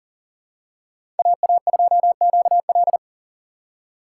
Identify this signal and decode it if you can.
Morse code, heard as a keyed tone, about 40 wpm, 700 Hz. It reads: AA2QL
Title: USA Call Signs - 5 Characters - 6x Character Spacing - 40wpm